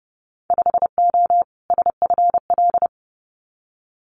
Morse code, heard as a keyed tone, about 30 wpm, 700 Hz. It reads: 5O SFL
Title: Winter Field Day Exchanges 30wpm